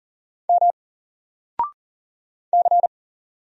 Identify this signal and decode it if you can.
Morse code, heard as a keyed tone, about 40 wpm, 700 Hz. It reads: M C
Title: Single Character (Letter:Number:Punctuation:Prosign) Alt 40wpm